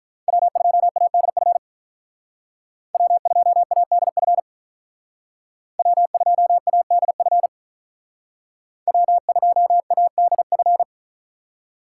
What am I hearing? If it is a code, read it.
Morse code, heard as a keyed tone, about 46 wpm, 700 Hz. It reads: W2ADF W2ADF W2ADF W2ADF